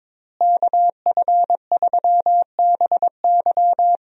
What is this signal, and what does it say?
Morse code, heard as a keyed tone, about 22 wpm, 700 Hz. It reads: KF3BY
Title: USA Call Signs - 5 Characters - Repeated 3x - 22wpm